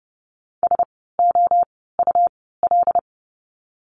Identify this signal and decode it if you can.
Morse code, heard as a keyed tone, about 30 wpm, spaced as for 17 wpm, 700 Hz. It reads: SOUL
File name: Lesson 17f - Letter u - Words 30wpm